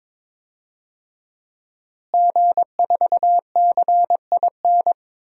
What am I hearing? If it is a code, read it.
Morse code, heard as a keyed tone, about 22 wpm, 700 Hz. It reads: G4CIN
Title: DXCC-100 Call Signs - 5 Characters - Repeated 2x - 22wpm